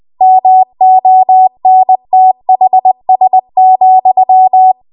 Morse code, heard as a keyed tone, about 20 wpm, 750 Hz. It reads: MONTHS,
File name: Archive / 20 WPM / 20WPM